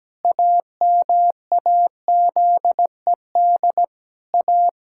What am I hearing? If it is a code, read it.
Morse code, heard as a keyed tone, about 17 wpm, 700 Hz. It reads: AMAZED A
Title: Pangram Challenge v2 - 17wpm